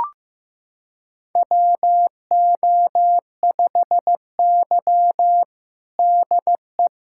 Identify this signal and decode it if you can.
Morse code, heard as a keyed tone, about 15 wpm, 700 Hz. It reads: WO5Y DE
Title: USA Callsign DE Callsign - 4 Characters - 15wpm